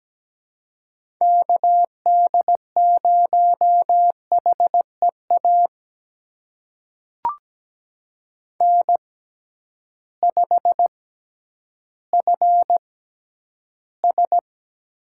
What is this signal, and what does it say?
Morse code, heard as a keyed tone, about 17 wpm, spaced as for 10 wpm, 700 Hz. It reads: KD0HEA N5FS
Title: USA Call Signs - 6x Character Spacing - 17wpm